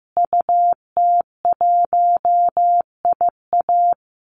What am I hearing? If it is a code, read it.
Morse code, heard as a keyed tone, about 15 wpm, 700 Hz. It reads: UT1IA
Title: Call Signs Repeated 3x - 15wpm